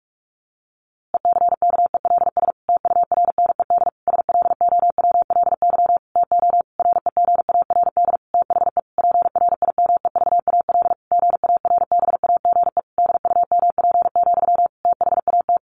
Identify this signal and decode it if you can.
Morse code, heard as a keyed tone, about 44 wpm, 700 Hz. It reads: ECKELS TURNED SLOWLY TO REGARD THE PRIMEVAL GARBAGE DUMP, THAT
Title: A Sound Of Thunder 45wpm